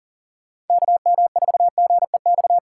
Morse code, heard as a keyed tone, about 49 wpm, 700 Hz. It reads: KM4GEX
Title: Speed Racing - USA 2x3 Call Signs 40wpm